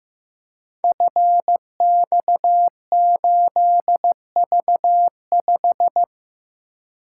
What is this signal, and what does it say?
Morse code, heard as a keyed tone, about 15 wpm, 700 Hz. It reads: FX8V5
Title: Code Groups - 5 Letter:Numbers - 15wpm